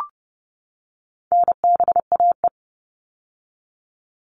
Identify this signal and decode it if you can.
Morse code, heard as a keyed tone, about 30 wpm, 700 Hz. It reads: NBAE
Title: All Cut Numbers - 4-Digit 30wpm